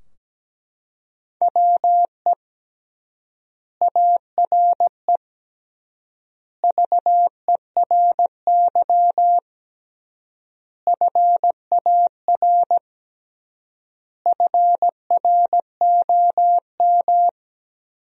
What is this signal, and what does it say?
Morse code, heard as a keyed tone, about 17 wpm, 700 Hz. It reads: WE ARE VERY FAR FROM
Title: 3x Word-Spacing - Sentences from Top 300 Words - 17wpm